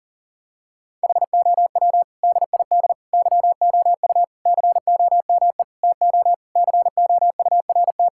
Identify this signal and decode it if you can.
Morse code, heard as a keyed tone, about 40 wpm, 700 Hz. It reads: HOW DID YOU COME TO COURT
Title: Sentences from Top 1600 Words 40wpm